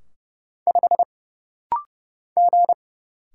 Morse code, heard as a keyed tone, about 30 wpm, 700 Hz. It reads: 5 G
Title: Single Letter-Number Alt - Rapid-Fire 30wpm